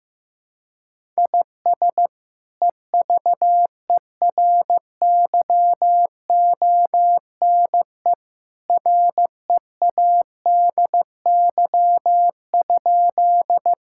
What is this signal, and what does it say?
Morse code, heard as a keyed tone, about 15 wpm, 700 Hz. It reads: IS EVERYONE READY?
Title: Sentences from Top 1000 Words - Review Part 5 - 15wpm